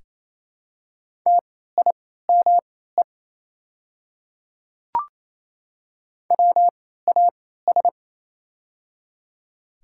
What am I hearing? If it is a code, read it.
Morse code, heard as a keyed tone, about 27 wpm, spaced as for 16 wpm, 700 Hz. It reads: TIME WAS
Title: 3x Character Spacing - Top 200 Words - 28wpm